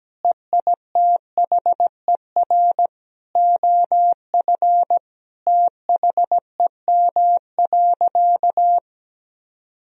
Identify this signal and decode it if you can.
Morse code, heard as a keyed tone, about 17 wpm, 700 Hz. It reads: EITHER OF THEM.
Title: Sentences from Top 1000 Words - Review Part 4 - 17wpm